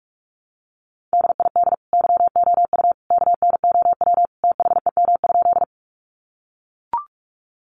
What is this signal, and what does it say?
Morse code, heard as a keyed tone, about 44 wpm, 700 Hz. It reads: DID YOU KNOW THEM?
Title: Sentences from Top 100 Words 45wpm